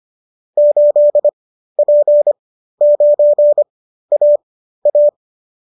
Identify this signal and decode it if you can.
Morse code, heard as a keyed tone, about 25 wpm, spaced as for 13 wpm, 590 Hz. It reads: 8P9AA